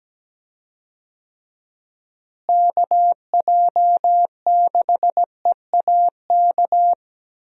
Morse code, heard as a keyed tone, about 17 wpm, 700 Hz. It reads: KJ6EAK
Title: USA Call Signs - 6 Characters - Repeated 2x - 17wpm